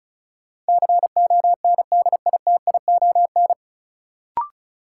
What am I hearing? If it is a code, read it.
Morse code, heard as a keyed tone, about 36 wpm, 700 Hz. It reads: CONDITION
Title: Sentences from Top 1000 Words - New Word Drill - 35wpm